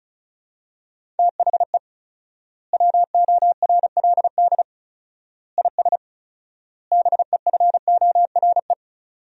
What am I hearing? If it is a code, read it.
Morse code, heard as a keyed tone, about 36 wpm, 700 Hz. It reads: THE WORLD IS BEFORE
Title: 4x Word-Spacing - Sentences from Top 200 Words - 35wpm